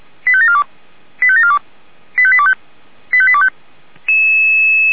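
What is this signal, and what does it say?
Five-tone selective call (ZVEI), heard as quick runs of five tones, tones of about 70 ms. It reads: 85742 85742 85825 85825
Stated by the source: ZVEI five-tone selective call 85827 received by SDR.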